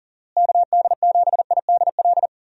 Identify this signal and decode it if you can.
Morse code, heard as a keyed tone, about 40 wpm, 700 Hz. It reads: KD7IDL
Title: USA Call Signs - 6 Characters - Repeated 2x - 40wpm